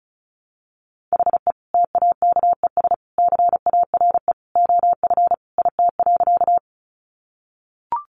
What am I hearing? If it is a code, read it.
Morse code, heard as a keyed tone, about 34 wpm, 700 Hz. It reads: HE TAKES CARE OF IT.
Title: Additional Easy Phrases 35wpm